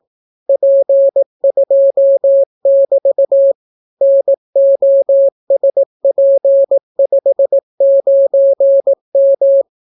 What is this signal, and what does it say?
Morse code, heard as a keyed tone, about 18 wpm, 550 Hz. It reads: P2<BT> NOSP59M